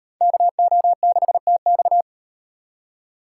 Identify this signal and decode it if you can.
Morse code, heard as a keyed tone, about 38 wpm, 700 Hz. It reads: KO6TX